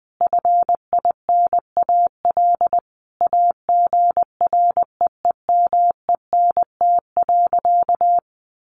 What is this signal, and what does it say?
Morse code, heard as a keyed tone, about 20 wpm, 700 Hz. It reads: FINAL AGREEMENT.
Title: Sentences from Top 1500 Words 20wpm